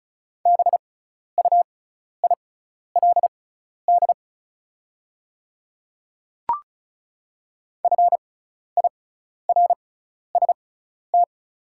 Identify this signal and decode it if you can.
Morse code, heard as a keyed tone, about 34 wpm, spaced as for 12 wpm, 700 Hz. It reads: BUILD FIRST
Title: 6x Character Spacing - Top 200 Words - 35wpm